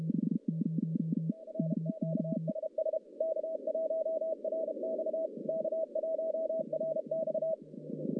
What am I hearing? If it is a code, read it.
Morse code, heard as a keyed tone, about 32 wpm, 640 Hz, with noise. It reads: R QSS K1RX K1RX
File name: contests / CW-SS-k5tr- -1200